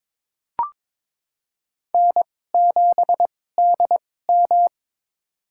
Morse code, heard as a keyed tone, about 22 wpm, spaced as for 16 wpm, 700 Hz. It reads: N7DM
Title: USA Call Signs - 4 Characters - 2x Character Spacing - 22wpm